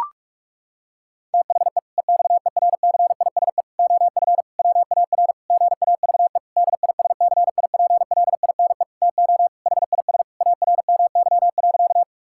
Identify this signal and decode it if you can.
Morse code, heard as a keyed tone, about 46 wpm, 700 Hz. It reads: THE EXERCISE OF WAR GAVE DISCIPLINE TO HIS ARMY.
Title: Sentences from Top 1000 Words - Review Part 6 - 45wpm